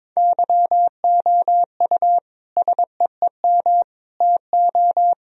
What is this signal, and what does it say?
Morse code, heard as a keyed tone, about 22 wpm, 700 Hz. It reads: YOU SEEM TO
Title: Sentences from Top 1000 Words - Review Part 1 - 22wpm